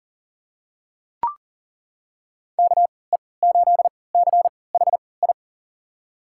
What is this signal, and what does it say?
Morse code, heard as a keyed tone, about 40 wpm, spaced as for 23 wpm, 700 Hz. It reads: KE8CHI
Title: USA Call Signs - 6 Characters - 3x Character Spacing - 40wpm